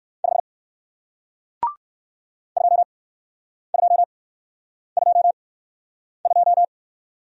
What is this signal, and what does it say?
Morse code, heard as a keyed tone, about 54 wpm, spaced as for 9 wpm, 700 Hz. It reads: 5 2222
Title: Speed-Racing Single Letter-Number 45wpm